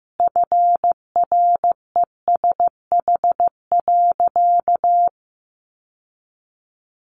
Morse code, heard as a keyed tone, about 14 wpm, 700 Hz. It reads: FRESH.
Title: Sentences from Top 1200 Words 15wpm